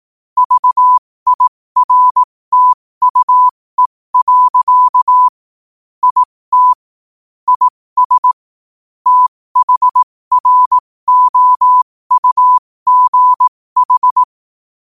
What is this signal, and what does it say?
Morse code, heard as a keyed tone, about 18 wpm, spaced as for 15 wpm, 1000 Hz. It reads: VIRTUE. IT IS THROUGH
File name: qotdfiles / 15wpm